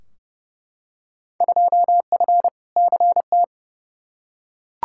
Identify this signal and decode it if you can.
Morse code, heard as a keyed tone, about 30 wpm, 700 Hz. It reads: 2F CT
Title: Field Day Exchanges Encore - 30wpm